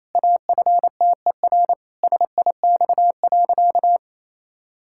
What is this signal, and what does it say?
Morse code, heard as a keyed tone, about 28 wpm, 700 Hz. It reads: AFTER SIX.
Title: Pangram Challenge v2 - 28wpm